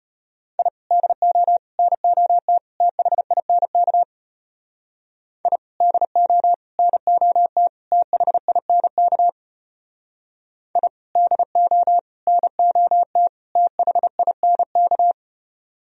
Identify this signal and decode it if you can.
Morse code, heard as a keyed tone, about 31 wpm, 700 Hz. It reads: I DO NOT THINK I DO NOT THINK I DO NOT THINK
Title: Speed-Racing - Sets of 4 Words 25wpm